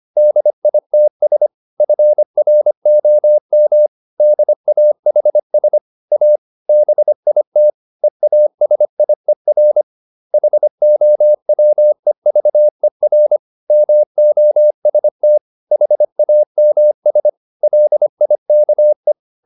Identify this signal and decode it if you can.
Morse code, heard as a keyed tone, about 25 wpm, 600 Hz. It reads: DITS FROM DAHS A BIT EASIER HOWEVER MOST HAMS LIKE